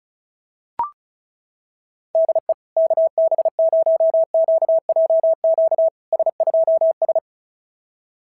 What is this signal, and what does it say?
Morse code, heard as a keyed tone, about 36 wpm, 660 Hz. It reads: DE KB0QJQ S2S
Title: SOTA QSO Practice 35wpm